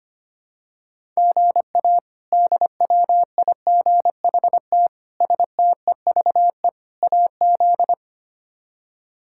Morse code, heard as a keyed tone, about 25 wpm, 700 Hz. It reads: GA DWIGHT STEVE AZ
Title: K1USN SST QSO Practice - 25wpm